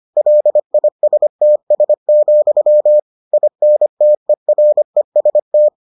Morse code, heard as a keyed tone, about 25 wpm, 600 Hz. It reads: LISTS, INTEREST